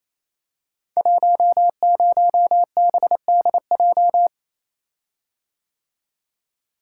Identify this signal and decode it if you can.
Morse code, heard as a keyed tone, about 28 wpm, 700 Hz. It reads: 10BDJ
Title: Code Groups - 5 Letter:Numbers - 28wpm